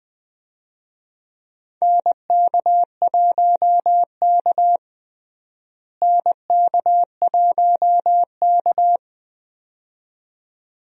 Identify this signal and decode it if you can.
Morse code, heard as a keyed tone, about 20 wpm, 700 Hz. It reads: NK1K NK1K
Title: USA Call Signs - 4 Characters - Repeated 2x - 20wpm